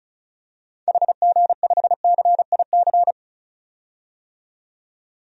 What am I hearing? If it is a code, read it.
Morse code, heard as a keyed tone, about 36 wpm, 700 Hz. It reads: HG5CIC
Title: DXCC-100 Call Signs - 6 Characters - 35wpm